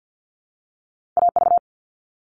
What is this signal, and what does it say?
Morse code, heard as a keyed tone, about 50 wpm, 700 Hz. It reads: A V
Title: All Cut Numbers - 2-Digit 50wpm